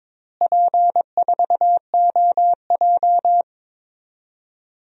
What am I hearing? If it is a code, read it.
Morse code, heard as a keyed tone, about 22 wpm, 700 Hz. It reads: P4OJ
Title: DXCC-100 Call Signs - 4 Characters - Repeated 3x - 22wpm